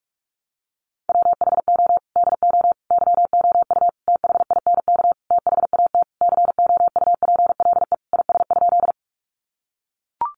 Morse code, heard as a keyed tone, about 44 wpm, 700 Hz. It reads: WHO DO YOU THINK THAT COUPLE IS?
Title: Sentences from Top 1100 Words 45wpm